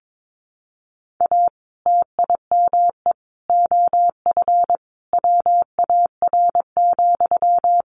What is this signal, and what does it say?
Morse code, heard as a keyed tone, about 22 wpm, 700 Hz. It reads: A TIME OF WAR,
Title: Sentences from Top 1500 Words 22wpm